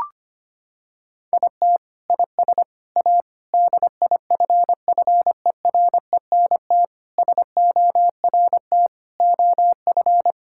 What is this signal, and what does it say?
Morse code, heard as a keyed tone, about 25 wpm, 700 Hz. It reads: IT IS A DIFFERENT SORT OF
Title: Sentences from Top 1000 Words - Review Part 5 - 25wpm